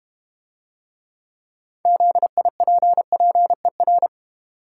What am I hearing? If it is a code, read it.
Morse code, heard as a keyed tone, about 32 wpm, 700 Hz. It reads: ZIPPER